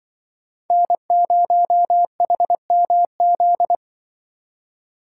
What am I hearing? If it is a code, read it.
Morse code, heard as a keyed tone, about 25 wpm, 700 Hz. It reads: N0HMZ
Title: Speed Racing - USA 1x3 Call Signs 20wpm